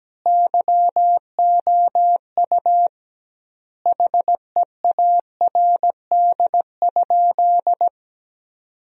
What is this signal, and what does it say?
Morse code, heard as a keyed tone, about 17 wpm, 700 Hz. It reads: YOU HEARD?
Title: 2x Word-Spacing - Sentences from Top 400 Words - 17wpm